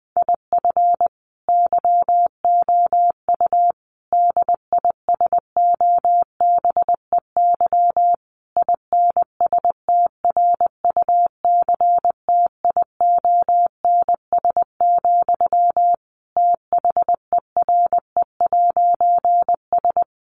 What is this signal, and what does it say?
Morse code, heard as a keyed tone, about 20 wpm, 700 Hz. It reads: IF YOU DISOBEY INSTRUCTIONS, THERE'S
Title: A Sound Of Thunder 20wpm